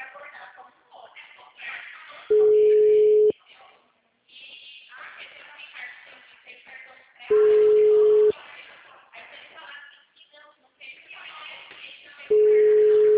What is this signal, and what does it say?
Ringback tone (Europe, 425 Hz), with speech.